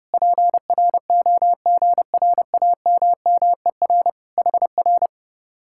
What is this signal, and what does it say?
Morse code, heard as a keyed tone, about 29 wpm, 700 Hz. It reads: PROGRAMMER HR